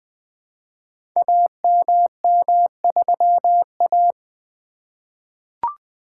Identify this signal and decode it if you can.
Morse code, heard as a keyed tone, about 20 wpm, 700 Hz. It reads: AMM3A
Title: Code Groups - 5 Letter:Numbers - 20wpm